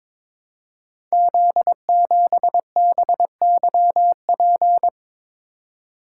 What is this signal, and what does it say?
Morse code, heard as a keyed tone, about 22 wpm, 700 Hz. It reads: Z7BYP
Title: Code Groups - 5 Letter:Numbers - 22wpm